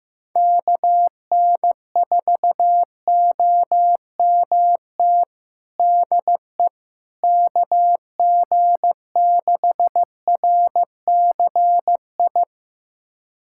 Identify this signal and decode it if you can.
Morse code, heard as a keyed tone, about 15 wpm, 700 Hz. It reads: KN4OMT DE KG6RCI
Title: USA Callsign DE Callsign - 6 Characters - 15wpm